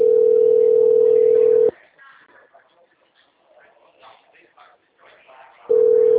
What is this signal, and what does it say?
Ringback tone (North America), 440 + 480 Hz, with speech.
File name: out- -2012